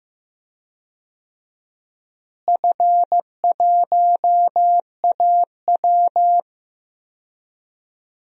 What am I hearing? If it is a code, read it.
Morse code, heard as a keyed tone, about 15 wpm, 700 Hz. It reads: F1AW
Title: DXCC-100 Call Signs - 4 Characters - Repeated 2x - 15wpm